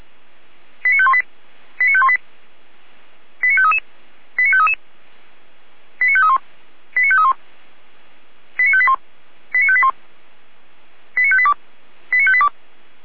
Five-tone selective call (ZVEI), heard as quick runs of five tones, tones of about 70 ms. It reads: 89518 89518 89533 89533 89531 89531 89581 89581 89582 89582